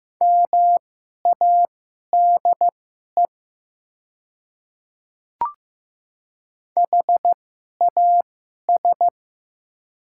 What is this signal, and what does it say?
Morse code, heard as a keyed tone, about 15 wpm, spaced as for 11 wpm, 700 Hz. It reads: MADE HAS
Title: 2x Character Spacing - Top 300 Words - 15wpm